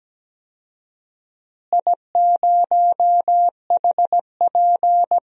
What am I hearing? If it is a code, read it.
Morse code, heard as a keyed tone, about 17 wpm, 700 Hz. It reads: I0HP
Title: DXCC-100 Call Signs - 4 Characters - Repeated 3x - 17wpm